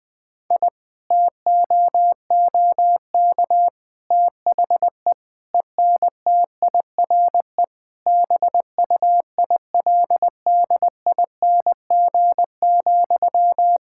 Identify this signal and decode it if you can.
Morse code, heard as a keyed tone, about 20 wpm, 700 Hz. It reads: I TOOK THE ENTIRE BUILDING,
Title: Sentences from Top 1000 Words - Review Part 5 - 20wpm